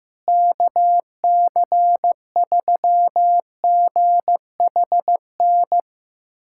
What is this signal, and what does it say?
Morse code, heard as a keyed tone, about 15 wpm, 700 Hz. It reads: KC3GHN
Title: USA Callsign DE Callsign - 6 Characters - 15wpm